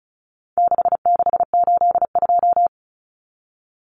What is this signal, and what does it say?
Morse code, heard as a keyed tone, about 34 wpm, 700 Hz. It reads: 6682
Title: Numbers - 4-Digit 35wpm